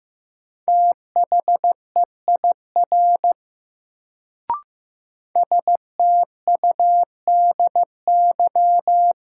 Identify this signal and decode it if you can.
Morse code, heard as a keyed tone, about 15 wpm, 700 Hz. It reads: THEIR STUDY
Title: Sentences from Top 300 Words - New Word Drill - 15wpm